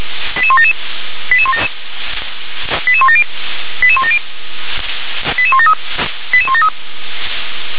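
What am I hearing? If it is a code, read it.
Five-tone selective call (ZVEI), heard as quick runs of five tones, tones of about 70 ms. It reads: 88177 88170 88170 88173 88173
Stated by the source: Five-tone ZVEI call 88020 received on an SDR.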